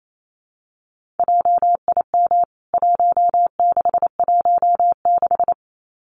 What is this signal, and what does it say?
Morse code, heard as a keyed tone, about 28 wpm, 700 Hz. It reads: JIM 1616
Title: CWT Exchange 28wpm